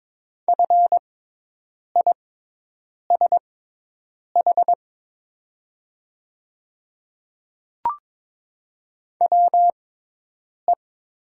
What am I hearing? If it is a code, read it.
Morse code, heard as a keyed tone, about 22 wpm, spaced as for 8 wpm, 700 Hz. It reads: FISH WE
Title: 6x Character Spacing - Top 300 Words - 22wpm